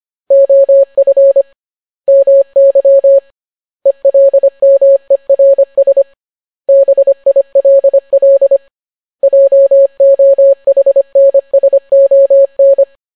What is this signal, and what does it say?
Morse code, heard as a keyed tone, about 25 wpm, 550 Hz. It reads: OF MY ELMERS BILL JOHNSON